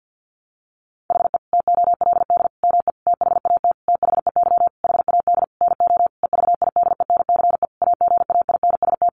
Morse code, heard as a keyed tone, about 50 wpm, 700 Hz. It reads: HE TOLD ME THAT THEY HAD NO EVIDENCE AGAINST
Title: Sentences from Top 1300 Words 50wpm